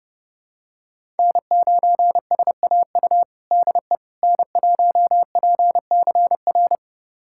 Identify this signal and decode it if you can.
Morse code, heard as a keyed tone, about 32 wpm, 700 Hz. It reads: N9SAU DE N1PCR
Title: USA Callsign DE Callsign - 5 Characters - 30wpm